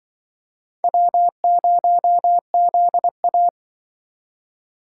Morse code, heard as a keyed tone, about 23 wpm, 700 Hz. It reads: W0ZA
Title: Speed Racing - USA 1x2 Call Signs 20wpm